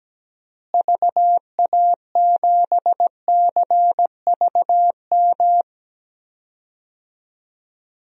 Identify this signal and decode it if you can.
Morse code, heard as a keyed tone, about 17 wpm, 700 Hz. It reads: VA7CVM
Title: DXCC-100 Call Signs - 6 Characters - Repeated 2x - 17wpm